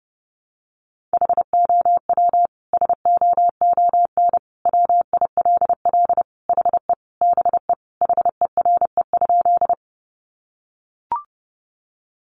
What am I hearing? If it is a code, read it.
Morse code, heard as a keyed tone, about 30 wpm, 700 Hz. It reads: HOW SOON WILL HE BE HERE?
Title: Sentences from Top 400 Words 30wpm